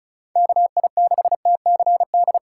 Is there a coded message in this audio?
Yes: Morse code from 0.4 to 2.4 s, heard as a keyed tone, about 44 wpm, 700 Hz. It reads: KI6TCD